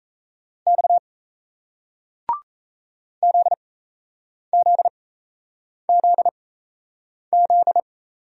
Morse code, heard as a keyed tone, about 38 wpm, spaced as for 8 wpm, 700 Hz. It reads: X ZZZZ